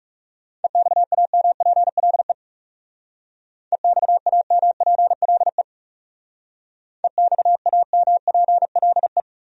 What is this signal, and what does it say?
Morse code, heard as a keyed tone, about 39 wpm, 700 Hz. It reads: EXAMPLE EXAMPLE EXAMPLE